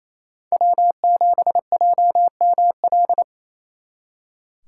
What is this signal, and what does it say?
Morse code, heard as a keyed tone, about 37 wpm, 700 Hz. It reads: W7JML